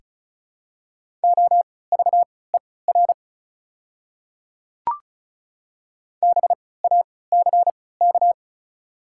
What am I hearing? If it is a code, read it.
Morse code, heard as a keyed tone, about 34 wpm, spaced as for 20 wpm, 700 Hz. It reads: OVER BACK
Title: 3x Character Spacing - Top 200 Words - 35wpm